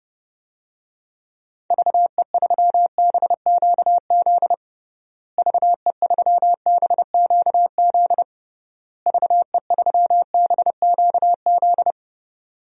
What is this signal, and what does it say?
Morse code, heard as a keyed tone, about 30 wpm, 700 Hz. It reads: VE3BQZ VE3BQZ VE3BQZ
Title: DXCC-100 Call Signs - 6 Characters - Repeated 3x - 30wpm